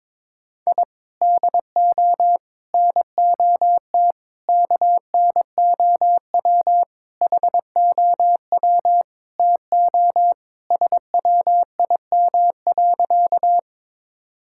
Morse code, heard as a keyed tone, about 22 wpm, 700 Hz. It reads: I DO NOT KNOW HOW TO SWIM.
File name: practice / 22wpm / Sentences with 4-Word Phrases - 22wpm